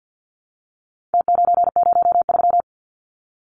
Morse code, heard as a keyed tone, about 48 wpm, 700 Hz. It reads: T903
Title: Numbers - Contest Style - 4-Digit 50wpm